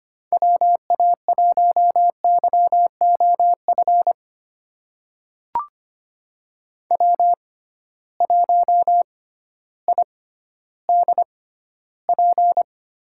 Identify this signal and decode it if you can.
Morse code, heard as a keyed tone, about 25 wpm, spaced as for 13 wpm, 700 Hz. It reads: WA1YOF W1IDP
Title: USA Call Signs - 6x Character Spacing - 25wpm